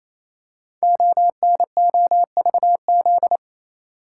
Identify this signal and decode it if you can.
Morse code, heard as a keyed tone, about 27 wpm, 700 Hz. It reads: ONOVZ
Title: Code Groups - 5 Letters - 28wpm